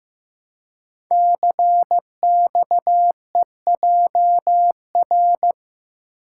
Morse code, heard as a keyed tone, about 15 wpm, 700 Hz. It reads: CXEJR
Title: Code Groups - 5 Letter:Numbers - 15wpm